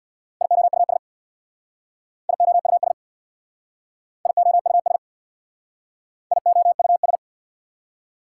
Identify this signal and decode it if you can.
Morse code, heard as a keyed tone, about 66 wpm, 700 Hz. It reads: IOUS IOUS IOUS IOUS